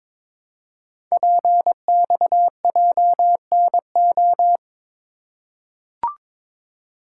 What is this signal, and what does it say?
Morse code, heard as a keyed tone, about 22 wpm, 700 Hz. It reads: PXJNO
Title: Code Groups - 5 Letters - 22wpm